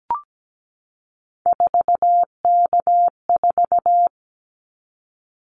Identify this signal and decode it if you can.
Morse code, heard as a keyed tone, about 17 wpm, 700 Hz. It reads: 4K4
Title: Callsign Prefixes for Top 100 DXCC - 17wpm